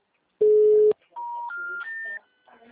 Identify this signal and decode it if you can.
Special information tone, with speech.
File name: out- -2016- - - .51269